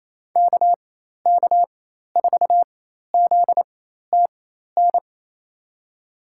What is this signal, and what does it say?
Morse code, heard as a keyed tone, about 28 wpm, spaced as for 13 wpm, 700 Hz. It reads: KK4ZTN
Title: USA Call Signs - 6 Characters - 4x Character Spacing - 28wpm